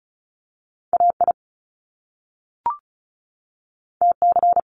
Morse code, heard as a keyed tone, about 36 wpm, 700 Hz. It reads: A I T C
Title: Top Two Letter Combinations in Words 35wpm